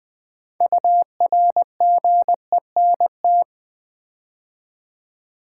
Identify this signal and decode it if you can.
Morse code, heard as a keyed tone, about 20 wpm, 700 Hz. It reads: URGENT
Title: 6 Letter Words - First 3 Letters Given - 20wpm